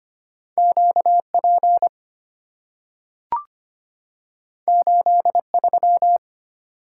Morse code, heard as a keyed tone, about 25 wpm, 700 Hz. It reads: Q P 8 3
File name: Two Letter-Number Combinations - 25wpm